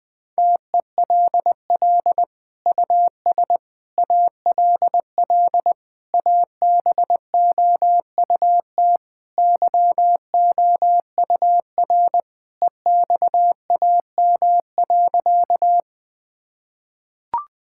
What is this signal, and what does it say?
Morse code, heard as a keyed tone, about 21 wpm, 700 Hz. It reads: TELL US ALL ABOUT YOUR EXAM.
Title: Sentences from Top 1000 Words - Review Part 5 - 20wpm